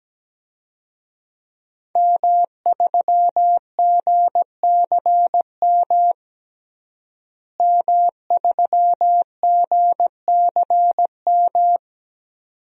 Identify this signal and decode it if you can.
Morse code, heard as a keyed tone, about 17 wpm, 700 Hz. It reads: M3GCM M3GCM
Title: DXCC-100 Call Signs - 5 Characters - Repeated 2x - 17wpm